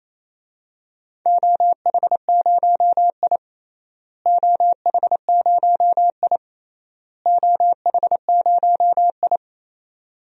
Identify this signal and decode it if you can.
Morse code, heard as a keyed tone, about 28 wpm, 700 Hz. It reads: OH0I OH0I OH0I
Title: DXCC-100 Call Signs - 4 Characters - Repeated 3x - 28wpm